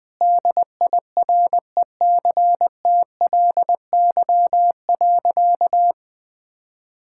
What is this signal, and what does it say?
Morse code, heard as a keyed tone, about 21 wpm, 700 Hz. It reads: DIRECTLY.
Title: Sentences from Top 1000 Words - Review Part 5 - 20wpm